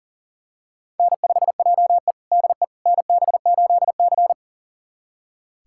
Morse code, heard as a keyed tone, about 37 wpm, 700 Hz. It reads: N5JE DE NB8C
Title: USA Callsign DE Callsign - 4 Characters - 40wpm